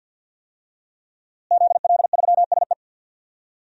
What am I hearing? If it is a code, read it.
Morse code, heard as a keyed tone, about 48 wpm, 700 Hz. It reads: ZL3SE
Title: DXCC-100 Call Signs - 5 Characters - Repeated 2x - 50wpm